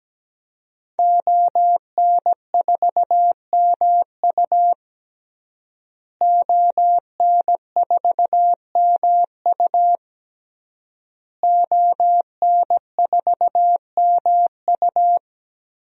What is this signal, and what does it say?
Morse code, heard as a keyed tone, about 17 wpm, 700 Hz. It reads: ON4MU ON4MU ON4MU
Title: DXCC-100 Call Signs - 5 Characters - Repeated 3x - 17wpm